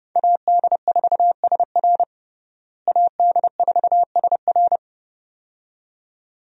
Morse code, heard as a keyed tone, about 30 wpm, 700 Hz. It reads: AD4SR AD4SR
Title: USA Call Signs - 5 Characters - Repeated 2x - 30wpm